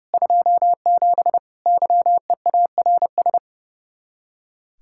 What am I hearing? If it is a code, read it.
Morse code, heard as a keyed tone, about 37 wpm, 700 Hz. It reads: 27 YEARS
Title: Speed-Racing Common QSO Phrases - 30wpm